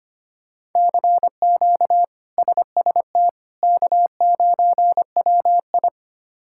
Morse code, heard as a keyed tone, about 25 wpm, 700 Hz. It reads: CQ SST K9WI
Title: K1USN SST QSO Practice - 25wpm